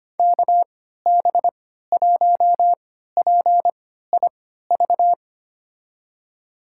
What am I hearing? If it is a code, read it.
Morse code, heard as a keyed tone, about 25 wpm, spaced as for 14 wpm, 700 Hz. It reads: KB1PIV
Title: USA Call Signs - 6 Characters - 3x Character Spacing - 25wpm